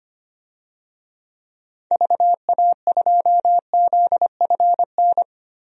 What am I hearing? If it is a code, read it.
Morse code, heard as a keyed tone, about 25 wpm, 700 Hz. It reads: VA2ZFN
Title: DXCC-100 Call Signs - 6 Characters - Repeated 3x - 25wpm